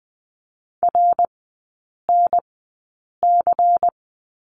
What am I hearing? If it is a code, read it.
Morse code, heard as a keyed tone, about 20 wpm, spaced as for 8 wpm, 700 Hz. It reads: RNC
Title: 198 Lesson 32e - Letter k - Three Characters 20wpm